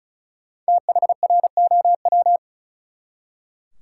Morse code, heard as a keyed tone, about 34 wpm, 700 Hz. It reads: THROW
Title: Five Letter Words - 35wpm